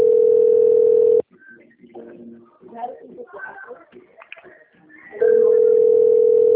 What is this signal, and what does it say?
Ringback tone (North America), 440 + 480 Hz, with speech.
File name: rg-1600-unknown- - - .33906